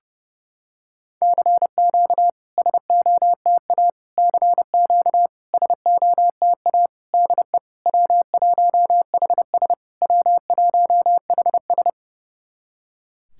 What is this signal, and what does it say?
Morse code, heard as a keyed tone, about 32 wpm, 700 Hz. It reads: CQ SOTA CQ SOTA DE W1HS W1HS
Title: SOTA QSO Practice 30wpm